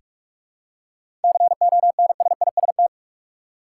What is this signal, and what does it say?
Morse code, heard as a keyed tone, about 46 wpm, 700 Hz. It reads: CONSIST
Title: Seven Letter Words - 45wpm